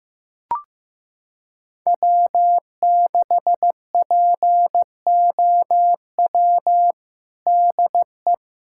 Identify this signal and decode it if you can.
Morse code, heard as a keyed tone, about 15 wpm, 700 Hz. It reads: W6POW DE
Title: USA Callsign DE Callsign 15wpm